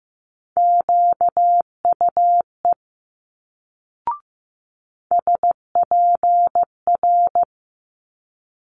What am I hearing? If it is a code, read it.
Morse code, heard as a keyed tone, about 15 wpm, 700 Hz. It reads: QUE SPR
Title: Multi-letter Phonemes Quick Fire 15wpm